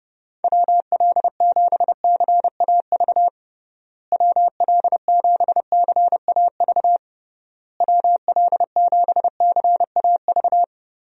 Morse code, heard as a keyed tone, about 30 wpm, 700 Hz. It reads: WL7CAV WL7CAV WL7CAV
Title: USA Call Signs - 6 Characters - Repeated 3x - 30wpm